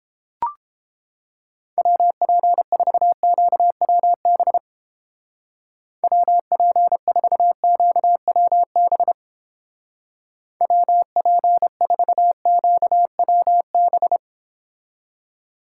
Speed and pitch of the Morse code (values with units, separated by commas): 30 wpm, 700 Hz